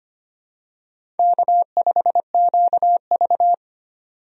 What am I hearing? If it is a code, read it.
Morse code, heard as a keyed tone, about 25 wpm, 700 Hz. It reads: K5QV
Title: USA Call Signs - 4 Characters - 25wpm